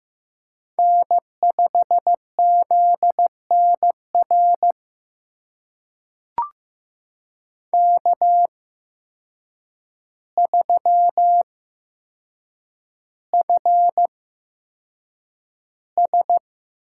Morse code, heard as a keyed tone, about 15 wpm, spaced as for 7 wpm, 700 Hz. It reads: N5ZNR K3FS
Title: USA Call Signs - 5 Characters - 8x Character Spacing - 15wpm